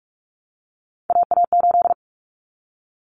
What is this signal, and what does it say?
Morse code, heard as a keyed tone, about 44 wpm, 700 Hz. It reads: AA8
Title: Common Cut Numbers - 3-Digit 45wpm